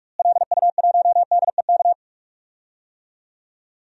Morse code, heard as a keyed tone, about 45 wpm, 700 Hz. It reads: LU1DEK
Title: DXCC-100 Call Signs - 6 Characters - Repeated 3x - 45wpm